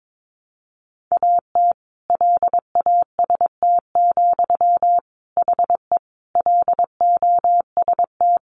Morse code, heard as a keyed tone, about 22 wpm, 700 Hz. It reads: AT LAST, HE LOST
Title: Sentences from Top 1400 Words 22wpm